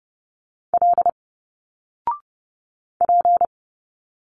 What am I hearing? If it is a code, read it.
Morse code, heard as a keyed tone, about 30 wpm, 700 Hz. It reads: L P
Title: Single Letters 30wpm